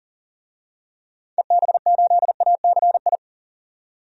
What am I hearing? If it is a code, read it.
Morse code, heard as a keyed tone, about 40 wpm, 700 Hz. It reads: EB8ACI
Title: DXCC-100 Call Signs - 6 Characters - 40wpm